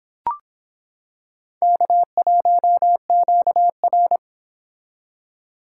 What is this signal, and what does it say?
Morse code, heard as a keyed tone, about 25 wpm, 700 Hz. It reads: K1QR